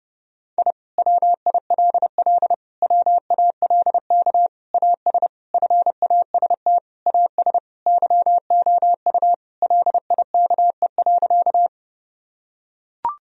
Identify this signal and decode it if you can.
Morse code, heard as a keyed tone, about 30 wpm, 700 Hz. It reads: I WILL WALK AS FAST AS YOU LIKE.
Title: Sentences from Top 1000 Words - Review Part 2 - 30wpm